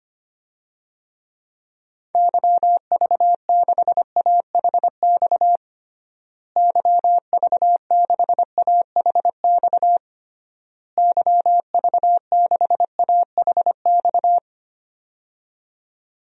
Morse code, heard as a keyed tone, about 25 wpm, 700 Hz. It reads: YV6AHX YV6AHX YV6AHX
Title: DXCC-100 Call Signs - 6 Characters - Repeated 3x - 25wpm